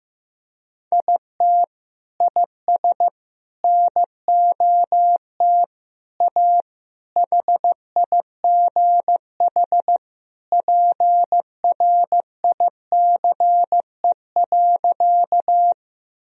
Morse code, heard as a keyed tone, about 16 wpm, 700 Hz. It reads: IT IS NOT A HIGH PRICE.
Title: Sentences from Top 1000 Words - Review Part 4 - 15wpm